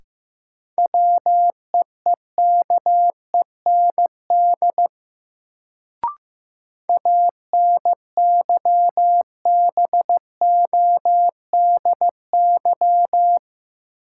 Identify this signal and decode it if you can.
Morse code, heard as a keyed tone, about 15 wpm, 700 Hz. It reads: WEEKEND ANYBODY
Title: Sets of 5 Words - New Word Drill - 15wpm